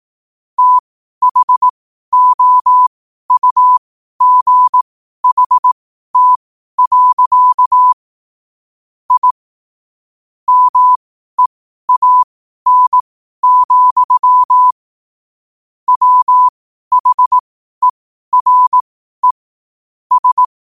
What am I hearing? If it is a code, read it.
Morse code, heard as a keyed tone, about 18 wpm, spaced as for 12 wpm, 1000 Hz. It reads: THOUGHT. I MEAN, WHERE S